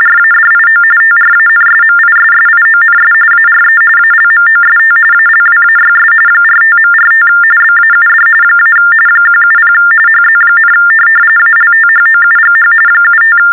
TTY/TDD text telephone tones, about 45.45 baud. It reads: RADIO TRANSMISSIONS ARE ALMOST ALWAYS 45.45 BAUD - 170 HZ, ALTHOUGH 75
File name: teletype-RTTY-45